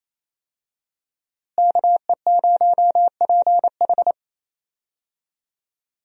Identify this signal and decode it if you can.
Morse code, heard as a keyed tone, about 28 wpm, 700 Hz. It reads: KE0PH
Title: USA Call Signs - 5 Characters - Repeated 3x - 28wpm